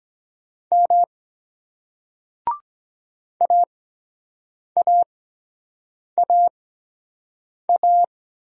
Morse code, heard as a keyed tone, about 23 wpm, spaced as for 7 wpm, 700 Hz. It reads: M AAAA